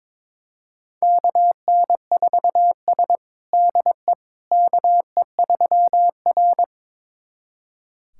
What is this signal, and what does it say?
Morse code, heard as a keyed tone, about 22 wpm, 700 Hz. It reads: KN4S DE KE3R
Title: USA Callsign DE Callsign - 4 Characters - 22wpm